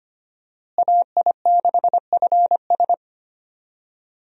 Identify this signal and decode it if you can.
Morse code, heard as a keyed tone, about 25 wpm, 700 Hz. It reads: AI6FS
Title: USA Call Signs - 5 Characters - Repeated 3x - 25wpm